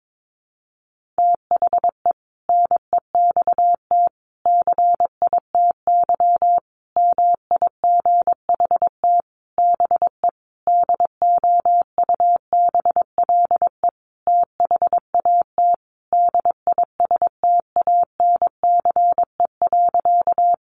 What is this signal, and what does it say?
Morse code, heard as a keyed tone, about 22 wpm, 700 Hz. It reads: THE NEXT CITY MIGHT BE DOUBLE THAT DISTANCE.
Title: Sentences from Top 1400 Words 22wpm